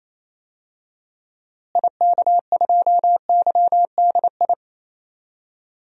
Morse code, heard as a keyed tone, about 28 wpm, 700 Hz. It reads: IK2YDI
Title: DXCC-100 Call Signs - 6 Characters - Repeated 3x - 28wpm